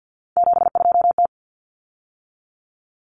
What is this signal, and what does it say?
Morse code, heard as a keyed tone, about 50 wpm, 700 Hz. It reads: 72T
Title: Common Cut Numbers - 3-Digit 50wpm